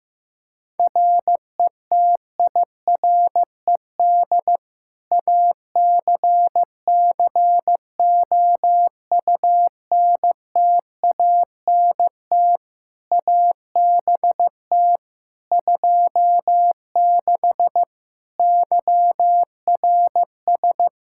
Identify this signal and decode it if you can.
Morse code, heard as a keyed tone, about 16 wpm, 700 Hz. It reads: RETIRED ACCOUNTANT ABT 26 YRS
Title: Practice QSO - Exchange 3 - More Info - 15wpm